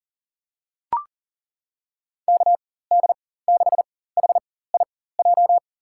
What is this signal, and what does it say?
Morse code, heard as a keyed tone, about 40 wpm, spaced as for 19 wpm, 700 Hz. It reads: KD6HIJ
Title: USA Call Signs - 6 Characters - 4x Character Spacing - 40wpm